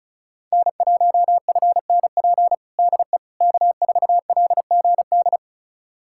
Morse code, heard as a keyed tone, about 34 wpm, 700 Hz. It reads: N1FNP DE K4LGD
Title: USA Callsign DE Callsign - 5 Characters - 35wpm